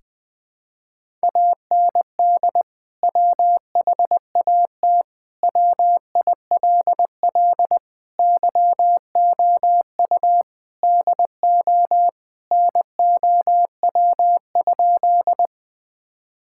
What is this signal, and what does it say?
Morse code, heard as a keyed tone, about 21 wpm, 700 Hz. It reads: AND WHAT WILL YOU DO NOW?
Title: Sentences with 2-Word Phrases - 20wpm